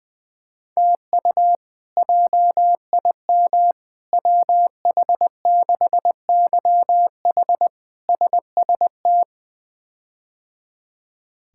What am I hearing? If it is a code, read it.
Morse code, heard as a keyed tone, about 20 wpm, 700 Hz. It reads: TU JIM WH6YH SST
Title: K1USN SST QSO Practice - 20wpm